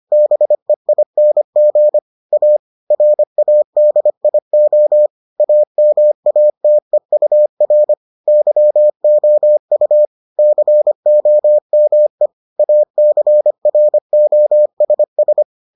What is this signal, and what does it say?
Morse code, heard as a keyed tone, about 25 wpm, 600 Hz. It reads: BEING A RADIO AMATEUR YOU COME ACROSS